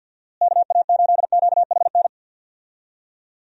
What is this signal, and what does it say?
Morse code, heard as a keyed tone, about 53 wpm, 700 Hz. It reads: KA8QHN
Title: USA Call Signs - 6 Characters - Repeated 3x - 50wpm